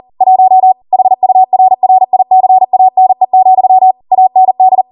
Morse code, heard as a keyed tone, about 40 wpm, 750 Hz. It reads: 1 HURRICANE, AND